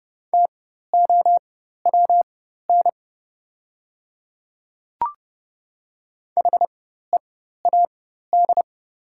Morse code, heard as a keyed tone, about 31 wpm, spaced as for 14 wpm, 700 Hz. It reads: TOWN HEAD
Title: 4x Character Spacing - Top 500 Words - 30wpm